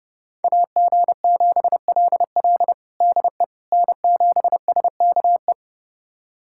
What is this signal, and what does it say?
Morse code, heard as a keyed tone, about 32 wpm, 700 Hz. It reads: AG7LL DE N7SKE
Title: USA Callsign DE Callsign - 5 Characters - 30wpm